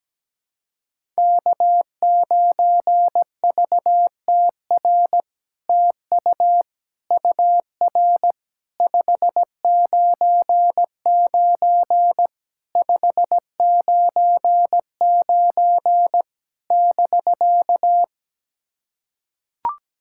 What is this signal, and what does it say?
Morse code, heard as a keyed tone, about 17 wpm, 700 Hz. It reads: K9VTR TU UR 599 599 <BK>
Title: SOTA QSO Practice 17wpm